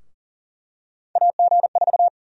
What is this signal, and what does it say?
Morse code, heard as a keyed tone, about 40 wpm, 700 Hz. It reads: AG4
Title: USA Call Sign Prefixes - Repeated 2x - 40wpm